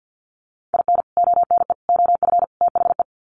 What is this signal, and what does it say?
Morse code, heard as a keyed tone, about 48 wpm, 700 Hz. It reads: IN ONE OF THE
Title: Sets of 4 Words 50wpm